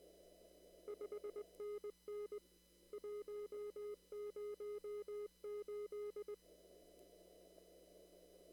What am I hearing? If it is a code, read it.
Morse code, heard as a keyed tone, about 20 wpm, 420 Hz. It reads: 5NN 108
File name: CW / kt5j wpx cw-op-k5pi